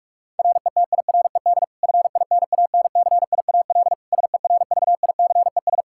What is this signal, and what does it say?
Morse code, heard as a keyed tone, about 46 wpm, 700 Hz. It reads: RETIRED FINANCIAL SERVICES
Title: K1USN SST QSO Practice 45wpm